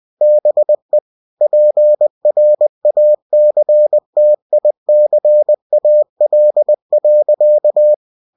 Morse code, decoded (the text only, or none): BE PRACTICAL.